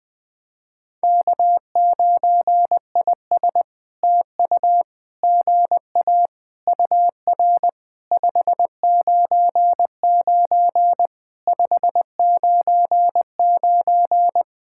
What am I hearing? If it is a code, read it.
Morse code, heard as a keyed tone, about 20 wpm, 700 Hz. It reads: K9IS TU GA UR 599 599
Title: POTA QSO Practice - 20wpm